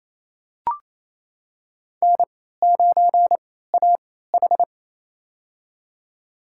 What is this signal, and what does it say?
Morse code, heard as a keyed tone, about 27 wpm, spaced as for 16 wpm, 700 Hz. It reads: N9AH
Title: USA Call Signs - 3x Character Spacing - 28wpm